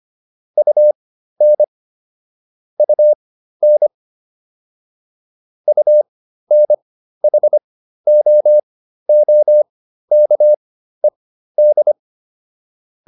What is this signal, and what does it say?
Morse code, heard as a keyed tone, about 25 wpm, spaced as for 13 wpm, 600 Hz. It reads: UN UN UNHOOKED